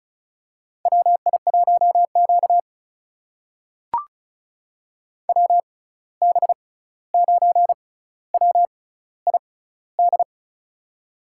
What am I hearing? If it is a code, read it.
Morse code, heard as a keyed tone, about 36 wpm, spaced as for 16 wpm, 700 Hz. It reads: WI1Q WB9WID
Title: USA Call Signs - 6x Character Spacing - 35wpm